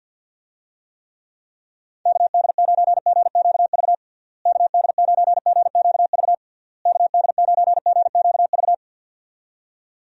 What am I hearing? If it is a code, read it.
Morse code, heard as a keyed tone, about 52 wpm, 700 Hz. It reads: KD9GQV KD9GQV KD9GQV
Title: USA Call Signs - 6 Characters - Repeated 3x - 50wpm